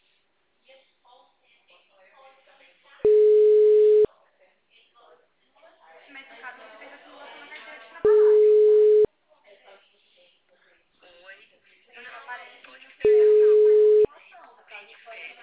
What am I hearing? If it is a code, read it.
Ringback tone (Europe, 425 Hz), with speech.